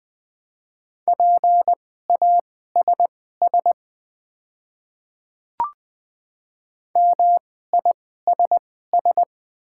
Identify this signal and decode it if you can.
Morse code, heard as a keyed tone, about 20 wpm, spaced as for 14 wpm, 700 Hz. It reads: PASS MISS
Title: 2x Character Spacing - Top 500 Words - 20wpm